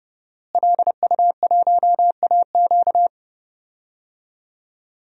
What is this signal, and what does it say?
Morse code, heard as a keyed tone, about 32 wpm, 700 Hz. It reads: LU1AQ
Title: DXCC-100 Call Signs - 5 Characters - Repeated 2x - 30wpm